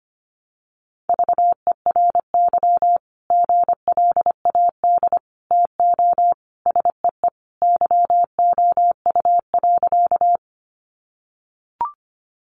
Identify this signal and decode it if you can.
Morse code, heard as a keyed tone, about 25 wpm, 700 Hz. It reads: VERY GLAD TO SEE YOU.
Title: Sentences from Top 800 Words 25wpm